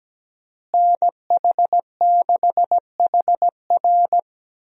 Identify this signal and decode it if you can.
Morse code, heard as a keyed tone, about 17 wpm, 700 Hz. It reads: NH6HR
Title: USA Call Signs - 5 Characters - Repeated 3x - 17wpm